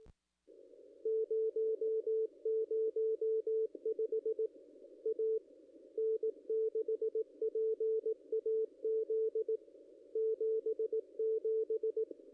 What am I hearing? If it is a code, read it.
Morse code, heard as a keyed tone, about 18 wpm, 440 Hz, with radio noise. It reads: 005 A N6PAZ 77
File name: contests / ss-cw-k5tr-op-n5rz -0200